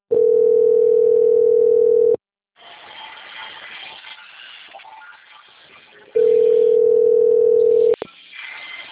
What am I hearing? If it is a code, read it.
Ringback tone (North America), 440 + 480 Hz.
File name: out- - - - - .3638